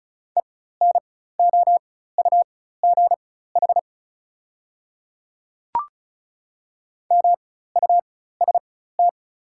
Morse code, heard as a keyed tone, about 36 wpm, spaced as for 16 wpm, 690 Hz. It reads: ENOUGH MUST
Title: 4x Character Spacing - Top 500 Words - 35wpm